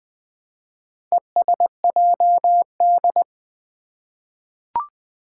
Morse code, heard as a keyed tone, about 20 wpm, 700 Hz. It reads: ESJD
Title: Four Challenging Letter-Number Combinations 20wpm